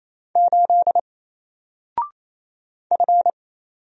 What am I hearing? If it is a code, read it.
Morse code, heard as a keyed tone, about 27 wpm, 700 Hz. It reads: 8 F
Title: Single Character (Letter:Number:Punctuation:Prosign) 28wpm